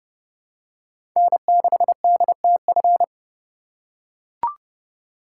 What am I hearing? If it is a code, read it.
Morse code, heard as a keyed tone, about 32 wpm, 700 Hz. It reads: N6DTF
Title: USA Call Signs - 5 Characters - 2x Character Spacing - 30wpm